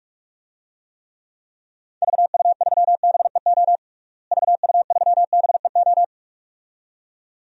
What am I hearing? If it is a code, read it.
Morse code, heard as a keyed tone, about 46 wpm, 700 Hz. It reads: VU3BEO VU3BEO
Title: DXCC-100 Call Signs - 6 Characters - Repeated 2x - 45wpm